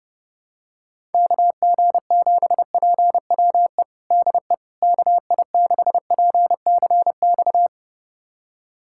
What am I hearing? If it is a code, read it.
Morse code, heard as a keyed tone, about 32 wpm, 700 Hz. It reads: KG7PWE DE KI6PCX
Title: USA Callsign DE Callsign - 6 Characters - 30wpm